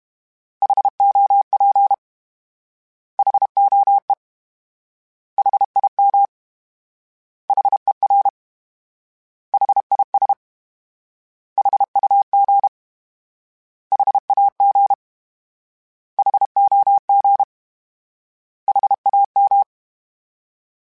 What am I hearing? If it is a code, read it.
Morse code, heard as a keyed tone, about 32 wpm, 800 Hz. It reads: HOP HOE HIM HER HIS HUG HAG HOG HAM
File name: H Prefixes / H mp3s